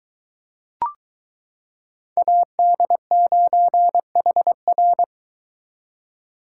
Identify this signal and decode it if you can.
Morse code, heard as a keyed tone, about 22 wpm, 700 Hz. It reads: AD9HR